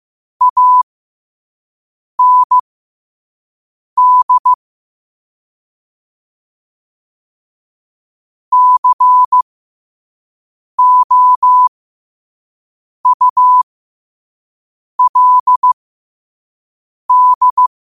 Morse code, heard as a keyed tone, about 15 wpm, spaced as for 5 wpm, 1000 Hz. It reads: AND COULD